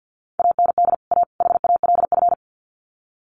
Morse code, heard as a keyed tone, about 50 wpm, 700 Hz. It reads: AND A HALF
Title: Sets of 3 Words 50wpm